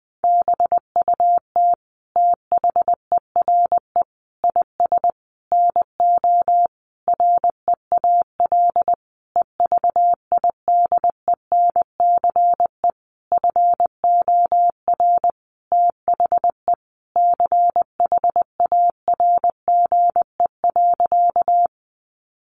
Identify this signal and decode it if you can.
Morse code, heard as a keyed tone, about 20 wpm, 700 Hz. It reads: BUT THERE IS NO REAL EVIDENCE FOR THE CHARGE.
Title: Sentences from Top 1300 Words 20wpm